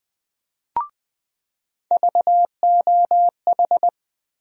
Morse code, heard as a keyed tone, about 20 wpm, 700 Hz. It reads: VOH
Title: USA Call Sign Postfixes - 3 Letters 20wpm